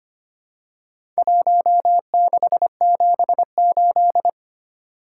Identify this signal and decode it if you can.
Morse code, heard as a keyed tone, about 25 wpm, 700 Hz. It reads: 1678
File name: practice / 25wpm / Four Letter-Number Combinations - 25wpm